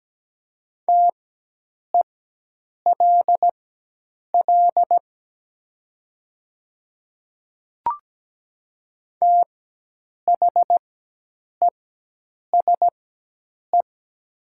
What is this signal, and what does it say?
Morse code, heard as a keyed tone, about 17 wpm, spaced as for 8 wpm, 700 Hz. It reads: TELL THESE
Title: 4x Character Spacing - Top 300 Words - 17wpm